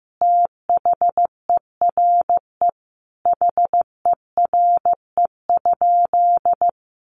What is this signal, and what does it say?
Morse code, heard as a keyed tone, about 15 wpm, 700 Hz. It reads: THERE HERE?
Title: Sentences from Top 1500 Words 15wpm